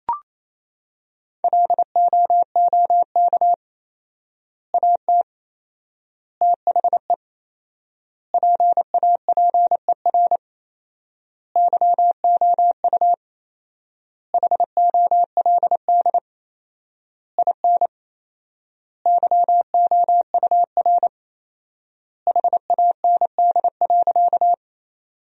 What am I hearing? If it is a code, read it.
Morse code, heard as a keyed tone, about 27 wpm, 700 Hz. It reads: LOOK AT THE PAPER YOU HOLD IN YOUR HAND.
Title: 4x Word-Spacing - Sentences from Top 400 Words - 28wpm